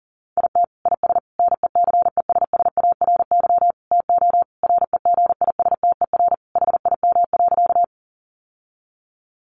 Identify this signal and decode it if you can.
Morse code, heard as a keyed tone, about 40 wpm, 700 Hz. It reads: IT IS NECESSARY TO REGISTER HIM.
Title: Sentences from Top 1200 Words 40wpm